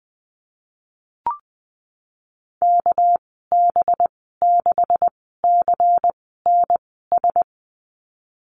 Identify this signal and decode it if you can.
Morse code, heard as a keyed tone, about 20 wpm, spaced as for 15 wpm, 700 Hz. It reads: KB6CNS
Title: 200 Lesson 32g - Letter k - Call Signs 20wpm